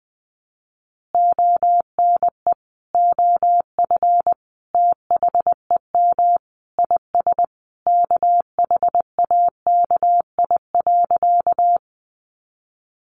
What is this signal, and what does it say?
Morse code, heard as a keyed tone, about 20 wpm, 700 Hz. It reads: ONE OF THEM IS KHAKI.
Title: Additional Easy Phrases 20wpm